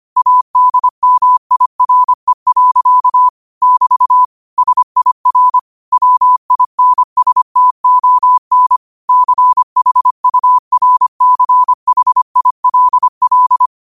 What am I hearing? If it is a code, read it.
Morse code, heard as a keyed tone, about 25 wpm, 1000 Hz. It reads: ADMIRE. <BT> SIR WINSTON CHURCHILL
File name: qotdfiles / 25wpm